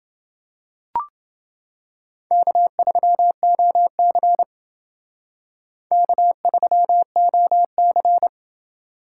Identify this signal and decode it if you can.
Morse code, heard as a keyed tone, about 28 wpm, 700 Hz. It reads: K3OC K3OC